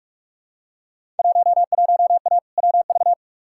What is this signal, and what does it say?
Morse code, heard as a keyed tone, about 43 wpm, 700 Hz. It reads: 11A WV
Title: Field Day Exchanges Encore - 45wpm